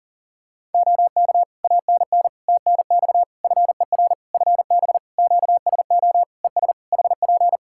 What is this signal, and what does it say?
Morse code, heard as a keyed tone, about 37 wpm, 700 Hz. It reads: OK ANN TNX FER FB QSO ES HP
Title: K1USN SST QSO Practice 40wpm